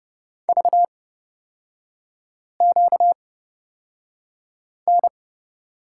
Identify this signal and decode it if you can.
Morse code, heard as a keyed tone, about 30 wpm, spaced as for 5 wpm, 690 Hz. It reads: VQN
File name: Lesson 38e - Letter q - Three Characters 30wpm